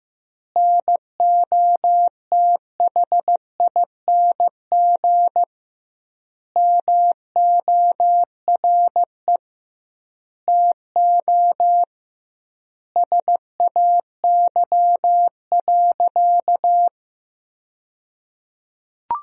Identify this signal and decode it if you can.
Morse code, heard as a keyed tone, about 15 wpm, 700 Hz. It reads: NOTHING MORE TO SAY.
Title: 2x Word-Spacing - Sentences from Top 500 Words - 15wpm